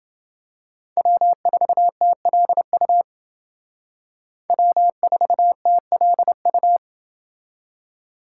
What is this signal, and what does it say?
Morse code, heard as a keyed tone, about 30 wpm, 700 Hz. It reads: W4TLU W4TLU